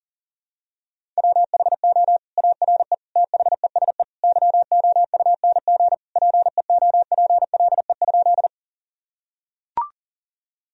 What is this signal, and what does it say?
Morse code, heard as a keyed tone, about 40 wpm, 700 Hz. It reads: WHO ARE THESE YOUNG PEOPLE?
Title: Sentences from Top 1000 Words - Review Part 1 - 40wpm